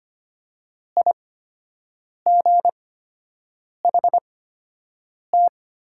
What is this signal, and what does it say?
Morse code, heard as a keyed tone, about 25 wpm, spaced as for 7 wpm, 700 Hz. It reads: IGHT
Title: 8x Character Spacing - Top 300 Words - 25wpm